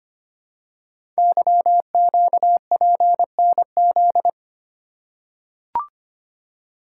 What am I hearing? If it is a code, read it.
Morse code, heard as a keyed tone, about 25 wpm, 700 Hz. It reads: YQPNZ
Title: Code Groups - 5 Letters - 25wpm